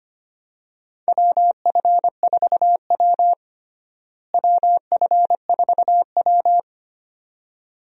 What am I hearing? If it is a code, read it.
Morse code, heard as a keyed tone, about 25 wpm, 700 Hz. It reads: WF4W WF4W
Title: USA Call Signs - 4 Characters - Repeated 2x - 25wpm